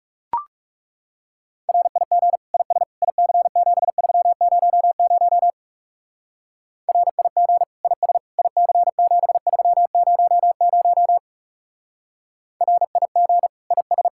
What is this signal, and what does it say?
Morse code, heard as a keyed tone, about 37 wpm, 700 Hz. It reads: RIG IS IC7300 RIG IS IC7300 RIG IS
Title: Speed-Racing Common QSO Phrases - 30wpm